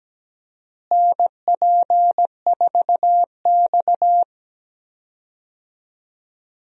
Morse code, heard as a keyed tone, about 17 wpm, 700 Hz. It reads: NP4X
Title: USA Call Signs - 4 Characters - Repeated 3x - 17wpm